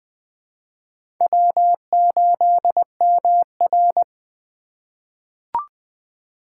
Morse code, heard as a keyed tone, about 20 wpm, 700 Hz. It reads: W8MR
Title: USA Call Signs - 3x Character Spacing - 20wpm